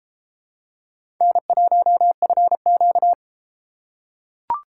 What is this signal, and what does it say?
Morse code, heard as a keyed tone, about 31 wpm, 700 Hz. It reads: N1FQ